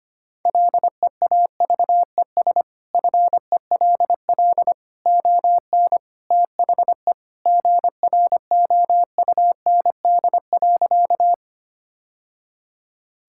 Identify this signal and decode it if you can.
Morse code, heard as a keyed tone, about 25 wpm, 700 Hz. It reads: LEAVES FELL ON THE GROUND.
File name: practice / 25wpm / Sentences with 3-Word Phrases - 25wpm